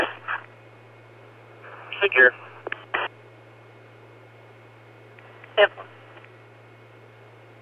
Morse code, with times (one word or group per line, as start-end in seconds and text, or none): none